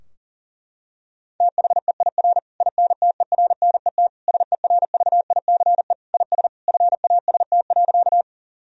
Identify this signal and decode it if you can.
Morse code, heard as a keyed tone, about 40 wpm, 700 Hz. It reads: THEIR INTERNET SERVICE IS FAST.
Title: Sentences from Top 1000 Words - Review Part 4 - 40wpm